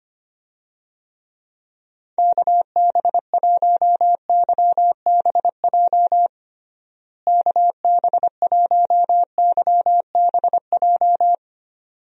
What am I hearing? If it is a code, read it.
Morse code, heard as a keyed tone, about 25 wpm, 700 Hz. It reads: KB1YBJ KB1YBJ
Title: USA Call Signs - 6 Characters - Repeated 2x - 25wpm